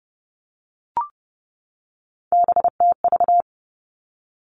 Morse code, heard as a keyed tone, about 30 wpm, 700 Hz. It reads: BTV
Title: Three Challenging Letter-Number Combinations 30wpm